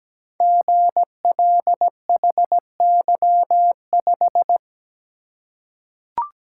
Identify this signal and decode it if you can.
Morse code, heard as a keyed tone, about 17 wpm, 700 Hz. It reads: GLHY5
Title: Code Groups - 5 Letter:Numbers - 17wpm